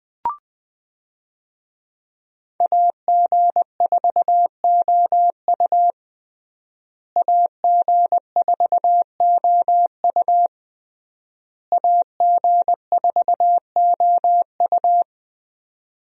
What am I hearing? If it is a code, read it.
Morse code, heard as a keyed tone, about 21 wpm, 700 Hz. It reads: AG4OU AG4OU AG4OU
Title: USA Call Signs - 5 Characters - Repeated 3x - 20wpm